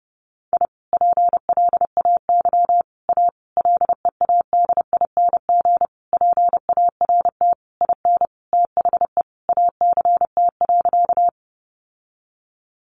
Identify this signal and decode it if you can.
Morse code, heard as a keyed tone, about 30 wpm, 700 Hz. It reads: I PLAY A LEADING PART IN THE ACT.
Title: Sentences from Top 1100 Words 30wpm